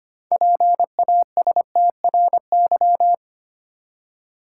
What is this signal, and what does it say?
Morse code, heard as a keyed tone, about 25 wpm, 690 Hz. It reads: PASTRY
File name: practice / 25wpm / Easy Words with Hints - 25wpm